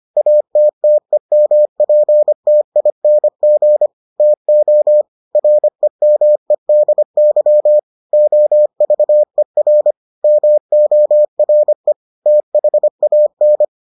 Morse code, heard as a keyed tone, about 25 wpm, 600 Hz. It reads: ATTEMPTING TO REMEDY OVER MORE THAN